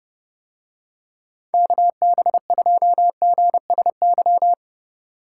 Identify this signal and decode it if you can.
Morse code, heard as a keyed tone, about 30 wpm, 700 Hz. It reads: KB2GSY